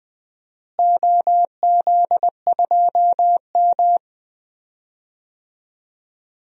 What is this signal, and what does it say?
Morse code, heard as a keyed tone, about 21 wpm, 700 Hz. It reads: OZ2M
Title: DXCC-100 Call Signs - 4 Characters - Repeated 2x - 20wpm